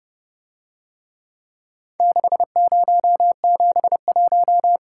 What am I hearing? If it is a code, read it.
Morse code, heard as a keyed tone, about 29 wpm, 700 Hz. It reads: 6071